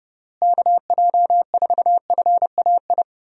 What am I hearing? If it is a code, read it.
Morse code, heard as a keyed tone, about 30 wpm, 700 Hz. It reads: KJ4FAI
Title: USA Call Signs - 6 Characters - Repeated 3x - 30wpm